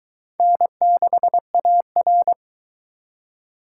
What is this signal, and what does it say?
Morse code, heard as a keyed tone, about 22 wpm, 700 Hz. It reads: N6AR